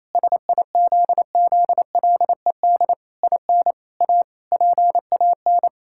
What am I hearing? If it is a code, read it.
Morse code, heard as a keyed tone, about 27 wpm, 700 Hz. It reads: SIZZLED IN A PAN
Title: Pangram Challenge v2 - 28wpm